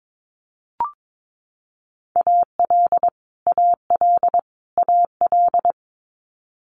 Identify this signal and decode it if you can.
Morse code, heard as a keyed tone, about 22 wpm, 700 Hz. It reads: AL AL AL
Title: Multi-letter Phonemes Repeated 3x 22wpm